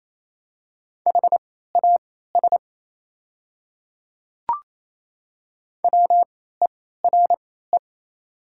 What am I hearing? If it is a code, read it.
Morse code, heard as a keyed tone, about 27 wpm, spaced as for 16 wpm, 690 Hz. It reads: HAS WERE
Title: 3x Character Spacing - Top 300 Words - 28wpm